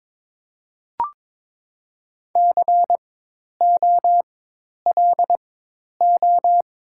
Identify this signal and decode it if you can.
Morse code, heard as a keyed tone, about 22 wpm, spaced as for 10 wpm, 700 Hz. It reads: COLO
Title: 4x Character Spacing - Top 500 Words - 22wpm